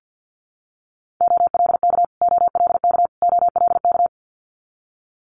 Morse code, heard as a keyed tone, about 50 wpm, 690 Hz. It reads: OLK OLK OLK
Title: Multi-letter Phonemes Repeated 3x 50wpm